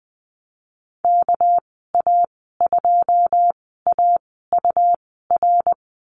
Morse code, heard as a keyed tone, about 20 wpm, spaced as for 14 wpm, 700 Hz. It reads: KA2AUR
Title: 213 Lesson 34g - Number 8 - Call Signs 20wpm